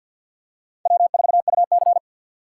Morse code, heard as a keyed tone, about 52 wpm, 700 Hz. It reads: W4UC
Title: USA Call Signs - 4 Characters - 3x Character Spacing - 50wpm